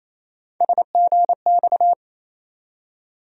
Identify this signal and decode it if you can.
Morse code, heard as a keyed tone, about 28 wpm, 700 Hz. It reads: SGX
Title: USA Call Sign Postfixes - 3 Letters 28wpm